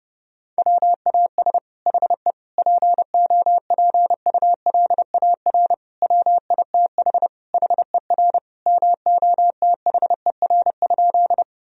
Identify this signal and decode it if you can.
Morse code, heard as a keyed tone, about 32 wpm, 700 Hz. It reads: WAS HE POPULAR WITH HER MOTHER?
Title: Sentences from Top 1000 Words - Review Part 3 - 30wpm